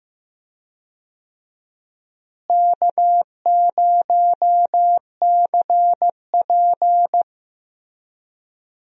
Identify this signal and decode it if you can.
Morse code, heard as a keyed tone, about 15 wpm, 700 Hz. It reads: K0CP
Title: USA Call Signs - 4 Characters - Repeated 3x - 15wpm